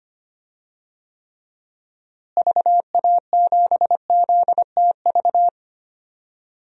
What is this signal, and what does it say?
Morse code, heard as a keyed tone, about 25 wpm, 700 Hz. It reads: VA7ZTV
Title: DXCC-100 Call Signs - 6 Characters - Repeated 2x - 25wpm